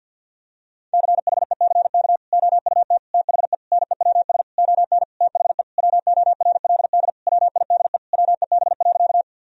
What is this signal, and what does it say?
Morse code, heard as a keyed tone, about 50 wpm, 700 Hz. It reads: CHECK OUT THE NEWS ON THE WORLD WIDE WEB.
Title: Sentences from Top 1000 Words - Review Part 4 - 50wpm